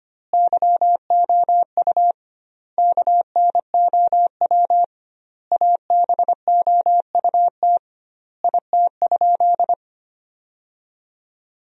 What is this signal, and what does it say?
Morse code, heard as a keyed tone, about 25 wpm, 700 Hz. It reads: YOU KNOW ABOUT IT?
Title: 2x Word-Spacing - Sentences from Top 100 Words - 25wpm